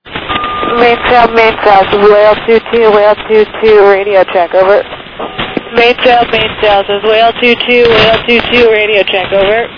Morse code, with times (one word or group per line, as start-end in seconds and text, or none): none